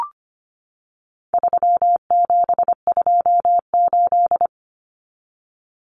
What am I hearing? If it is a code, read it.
Morse code, heard as a keyed tone, about 25 wpm, 700 Hz. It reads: 3728
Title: Common Cut Numbers - 4-Digit 25wpm